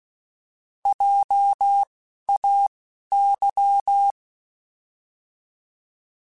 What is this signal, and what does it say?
Morse code, heard as a keyed tone, about 16 wpm, spaced as for 11 wpm, 800 Hz. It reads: JAY
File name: Alphabetical / Alphabetical mp3s / Js @10